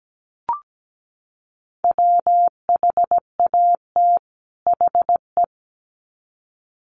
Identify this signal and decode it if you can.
Morse code, heard as a keyed tone, about 17 wpm, 700 Hz. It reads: WHAT HE
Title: Sets of 2 Words 17wpm